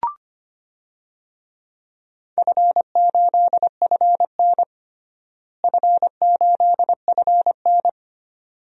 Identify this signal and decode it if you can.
Morse code, heard as a keyed tone, about 25 wpm, 700 Hz. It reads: F8FN F8FN
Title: DXCC-100 Call Signs - 4 Characters - Repeated 2x - 25wpm